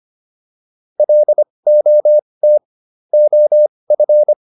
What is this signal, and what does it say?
Morse code, heard as a keyed tone, about 25 wpm, spaced as for 20 wpm, 600 Hz. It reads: LOT OF